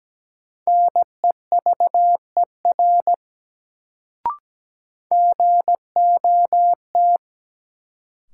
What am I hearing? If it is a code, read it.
Morse code, heard as a keyed tone, about 17 wpm, 700 Hz. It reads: NEVER GOT
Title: Sentences from Top 300 Words - New Word Drill - 17wpm